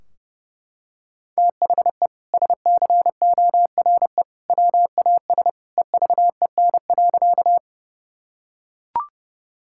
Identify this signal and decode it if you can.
Morse code, heard as a keyed tone, about 32 wpm, 700 Hz. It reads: THE SCORE WAS EVEN.
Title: Sentences from Top 1600 Words 30wpm